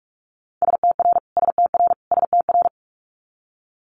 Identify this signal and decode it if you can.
Morse code, heard as a keyed tone, about 46 wpm, 700 Hz. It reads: STR STR STR
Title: Multi-letter Phonemes Repeated 3x 45wpm